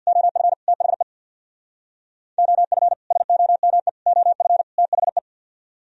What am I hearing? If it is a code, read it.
Morse code, heard as a keyed tone, about 64 wpm, 700 Hz. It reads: OF THE OF SOME OF THE